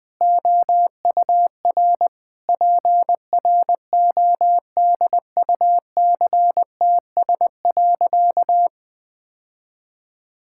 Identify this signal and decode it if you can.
Morse code, heard as a keyed tone, about 20 wpm, 700 Hz. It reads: OUR PRODUCTS.
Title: Sentences with 6-Word Phrases - 20wpm